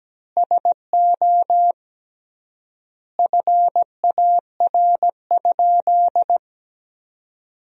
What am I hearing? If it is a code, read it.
Morse code, heard as a keyed tone, about 17 wpm, 700 Hz. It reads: SO FAR?
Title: 3x Word-Spacing - Sentences from Top 300 Words - 17wpm